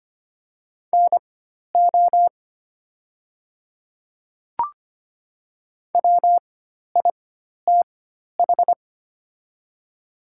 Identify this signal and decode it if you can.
Morse code, heard as a keyed tone, about 25 wpm, spaced as for 12 wpm, 700 Hz. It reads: NO WITH
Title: 4x Character Spacing - Top 100 Words - 25wpm